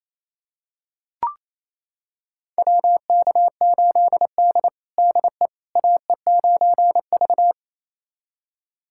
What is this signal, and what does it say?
Morse code, heard as a keyed tone, about 27 wpm, 700 Hz. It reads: WK8D DE AE9V
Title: USA Callsign DE Callsign - 4 Characters - 28wpm